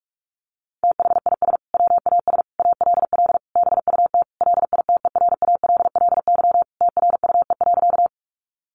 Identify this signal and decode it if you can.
Morse code, heard as a keyed tone, about 44 wpm, 700 Hz. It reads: THIS WAS ALL BUT LITERALLY TRUE.
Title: Sentences from Top 1400 Words 45wpm